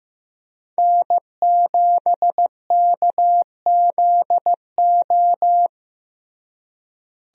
Morse code, heard as a keyed tone, about 15 wpm, 700 Hz. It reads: N7KZO
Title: USA Call Signs - 5 Characters - Repeated 3x - 15wpm